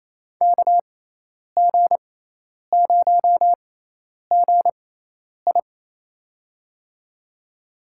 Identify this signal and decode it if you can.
Morse code, heard as a keyed tone, about 28 wpm, spaced as for 10 wpm, 700 Hz. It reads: KG0GI
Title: USA Call Signs - 5 Characters - 6x Character Spacing - 28wpm